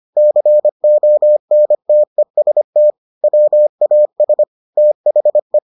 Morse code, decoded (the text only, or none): CONTEST WAS THE